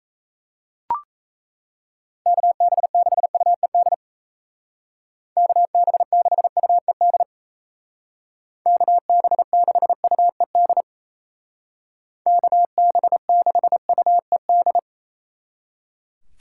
Morse code, heard as a keyed tone, about 35 wpm, 700 Hz. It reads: KB6UED KB6UED KB6UED KB6UED
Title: Speed Racing - USA 2x3 Call Signs 28wpm